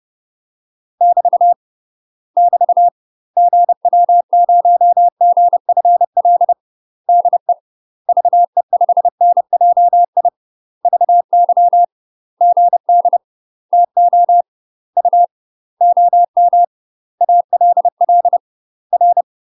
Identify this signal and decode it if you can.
Morse code, heard as a keyed tone, about 30 wpm, 700 Hz. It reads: <BT> <BT> GW0GFL DE VE5NJI VY GD TO U OM ALL R
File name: UK MASTER QSOs TRAINING / 30WPM / UK MASTER QSOs 0051